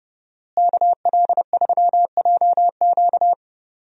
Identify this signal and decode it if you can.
Morse code, heard as a keyed tone, about 32 wpm, 700 Hz. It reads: KL3JQ
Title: USA Call Signs - 5 Characters - Repeated 2x - 30wpm